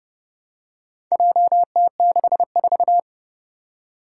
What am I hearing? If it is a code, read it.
Morse code, heard as a keyed tone, about 30 wpm, 700 Hz. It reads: JT64
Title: Four Challenging Letter-Number Combinations 30wpm